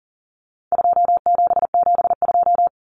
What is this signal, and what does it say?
Morse code, heard as a keyed tone, about 40 wpm, 700 Hz. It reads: 2772
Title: Common Cut Numbers - 4-Digit 40wpm